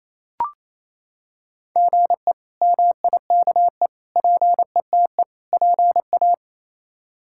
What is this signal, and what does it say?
Morse code, heard as a keyed tone, about 28 wpm, 700 Hz. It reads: GE MIKE PETE PA
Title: K1USN SST QSO Practice - 28wpm